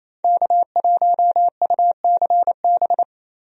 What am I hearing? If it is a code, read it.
Morse code, heard as a keyed tone, about 27 wpm, 700 Hz. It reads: K1UCB
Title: USA Call Signs - 5 Characters - Repeated 2x - 28wpm